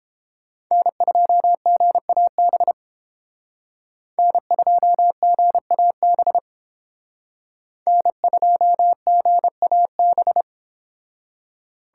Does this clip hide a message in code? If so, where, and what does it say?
Yes: Morse code from 0.7 to 10.4 s, heard as a keyed tone, about 30 wpm, 700 Hz. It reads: N2GAB N2GAB N2GAB